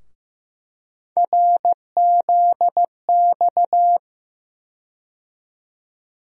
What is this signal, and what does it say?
Morse code, heard as a keyed tone, about 15 wpm, 700 Hz. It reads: RZX
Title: USA Call Sign Postfixes - 3 Letters 15wpm